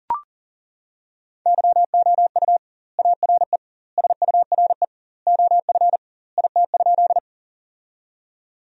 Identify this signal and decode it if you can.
Morse code, heard as a keyed tone, about 40 wpm, 700 Hz. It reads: YOU ARE SURE OF IT?
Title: 2x Word-Spacing - Sentences from Top 300 Words - 40wpm